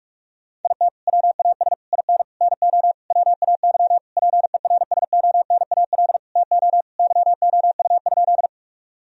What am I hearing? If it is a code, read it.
Morse code, heard as a keyed tone, about 46 wpm, 700 Hz. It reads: IT WAS IN NO WAY PERSONAL TO YOU?
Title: Sentences from Top 1000 Words - Review Part 4 - 45wpm